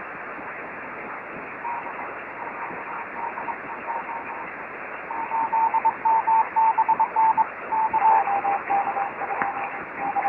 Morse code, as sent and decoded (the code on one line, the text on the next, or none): none